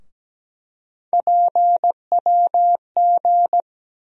Morse code, heard as a keyed tone, about 17 wpm, 700 Hz. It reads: PWG
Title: USA Call Sign Postfixes - 3 Letters 17wpm